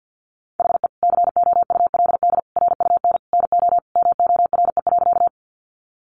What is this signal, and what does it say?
Morse code, heard as a keyed tone, about 50 wpm, 700 Hz. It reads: HE COULD RUN NO MORE.
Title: Sentences from Top 300 Words 50wpm